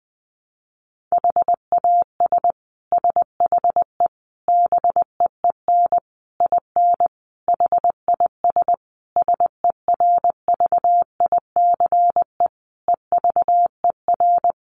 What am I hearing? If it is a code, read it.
Morse code, heard as a keyed tone, about 20 wpm, 700 Hz. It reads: HAS SHE BEEN IN HIS SERVICE EVER
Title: Sentences from Top 700 Words 20wpm